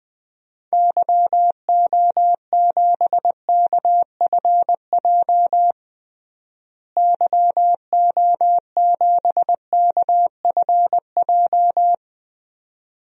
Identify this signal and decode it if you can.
Morse code, heard as a keyed tone, about 20 wpm, 700 Hz. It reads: YO7KFJ YO7KFJ
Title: DXCC-100 Call Signs - 6 Characters - Repeated 2x - 20wpm